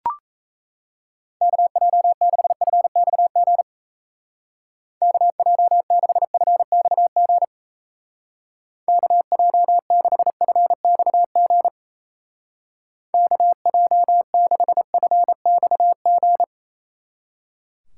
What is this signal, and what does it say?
Morse code, heard as a keyed tone, about 35 wpm, 700 Hz. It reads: KJ6FXG KJ6FXG KJ6FXG KJ6FXG